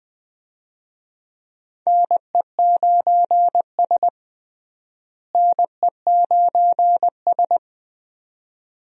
Morse code, heard as a keyed tone, about 21 wpm, 700 Hz. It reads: NE9S NE9S
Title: USA Call Signs - 4 Characters - Repeated 2x - 20wpm